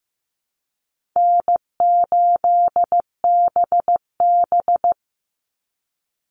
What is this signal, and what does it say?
Morse code, heard as a keyed tone, about 15 wpm, 700 Hz. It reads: N8BB
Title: NA SOTA Chaser Callsigns 15wpm